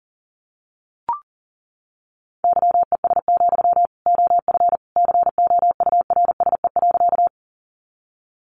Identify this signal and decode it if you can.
Morse code, heard as a keyed tone, about 37 wpm, 700 Hz. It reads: YES, OF COURSE.
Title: Sentences from Top 500 Words 40wpm